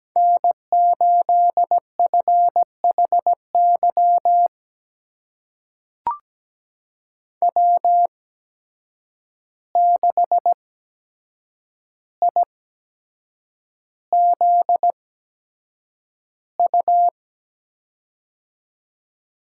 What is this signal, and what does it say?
Morse code, heard as a keyed tone, about 17 wpm, spaced as for 7 wpm, 700 Hz. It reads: N8FHY W6IZU
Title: USA Call Signs - 5 Characters - 8x Character Spacing - 17wpm